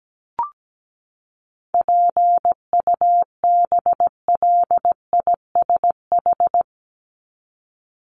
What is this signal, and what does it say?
Morse code, heard as a keyed tone, about 17 wpm, 700 Hz. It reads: PUBLISH
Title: ED Suffix - Top 100 Words with Base Words 17wpm